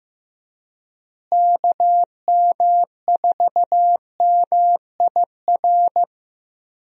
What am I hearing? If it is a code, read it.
Morse code, heard as a keyed tone, about 15 wpm, 700 Hz. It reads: KM4MIR
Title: USA Call Signs - 6 Characters - Repeated 2x - 15wpm